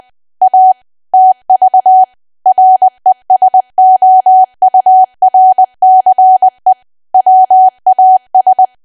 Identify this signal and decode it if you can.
Morse code, heard as a keyed tone, about 20 wpm, 750 Hz. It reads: A TV RESOURCE WAS
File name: Archive / 20 WPM / 20WPM